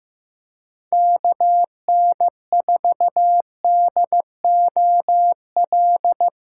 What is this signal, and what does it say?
Morse code, heard as a keyed tone, about 15 wpm, 700 Hz. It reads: KN4DOL
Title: USA Call Signs - 4x Character Spacing - 15wpm